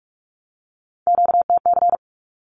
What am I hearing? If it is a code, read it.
Morse code, heard as a keyed tone, about 44 wpm, 700 Hz. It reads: QTC
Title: Q-Codes 45wpm